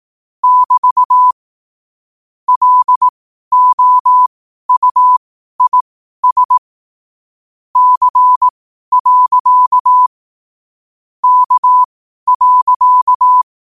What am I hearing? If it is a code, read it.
Morse code, heard as a keyed tone, about 18 wpm, spaced as for 12 wpm, 1000 Hz. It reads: <BT> LOUIS C. K.
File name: qotdfiles / 12wpm